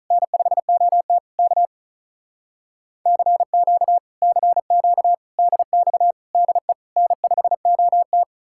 Morse code, heard as a keyed tone, about 46 wpm, 700 Hz. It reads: N5OT K CQ CQ DX DE N5OT